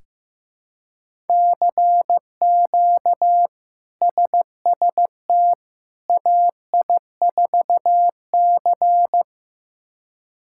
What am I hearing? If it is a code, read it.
Morse code, heard as a keyed tone, about 15 wpm, 700 Hz. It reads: CQ SST AI4C
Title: K1USN SST QSO Practice - 15wpm